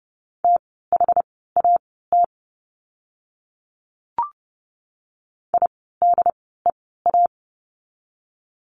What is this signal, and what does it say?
Morse code, heard as a keyed tone, about 30 wpm, spaced as for 17 wpm, 700 Hz. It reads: THAT IDEA
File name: Lesson 16f - Letter c - Words 30wpm